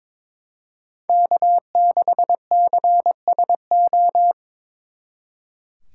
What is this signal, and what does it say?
Morse code, heard as a keyed tone, about 26 wpm, 700 Hz. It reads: K6CSO
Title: Speed Racing - USA 1x3 Call Signs 22wpm